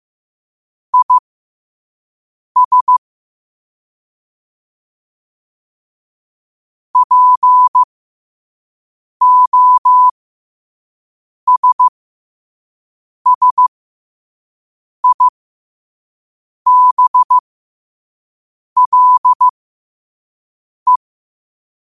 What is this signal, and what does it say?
Morse code, heard as a keyed tone, about 15 wpm, spaced as for 5 wpm, 1000 Hz. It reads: IS POSSIBLE
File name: qotdfiles / 5wpm